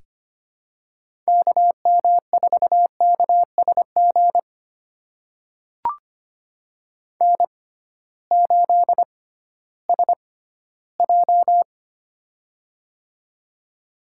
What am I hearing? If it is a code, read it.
Morse code, heard as a keyed tone, about 25 wpm, spaced as for 15 wpm, 700 Hz. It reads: KM4KSG N8SJ
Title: USA Call Signs - 6x Character Spacing - 25wpm